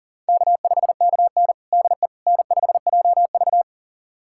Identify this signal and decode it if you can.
Morse code, heard as a keyed tone, about 40 wpm, 700 Hz. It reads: K5KN DE N5JV
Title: USA Callsign DE Callsign - 4 Characters - 40wpm